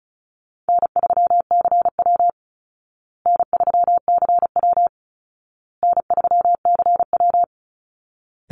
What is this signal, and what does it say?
Morse code, heard as a keyed tone, about 34 wpm, 700 Hz. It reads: N3CW N3CW N3CW
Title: Easy Call Signs Repeated 3x - 35wpm